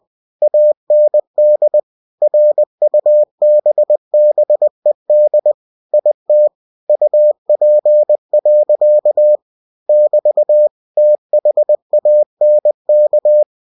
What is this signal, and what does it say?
Morse code, heard as a keyed tone, about 20 wpm, 600 Hz. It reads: AND RUBBED IT UP. <BT> THANK
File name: cw-books / winnie-the-pooh / winnie-20wpm-page-0088